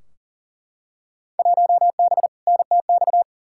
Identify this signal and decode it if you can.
Morse code, heard as a keyed tone, about 40 wpm, 700 Hz. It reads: 1B NTX
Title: Field Day Exchanges - 40wpm